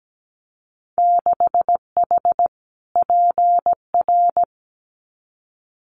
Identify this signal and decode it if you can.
Morse code, heard as a keyed tone, about 17 wpm, 700 Hz. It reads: 6H PR
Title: Winter Field Day Exchanges Encore 17wpm